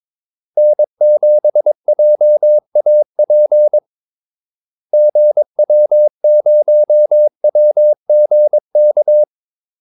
Morse code, heard as a keyed tone, about 22 wpm, 600 Hz. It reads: N7JAP GW0WGK
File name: Callsigns-22wpm